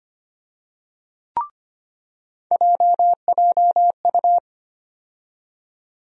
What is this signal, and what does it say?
Morse code, heard as a keyed tone, about 25 wpm, 700 Hz. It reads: JJU
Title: USA Call Sign Postfixes - 3 Letters 25wpm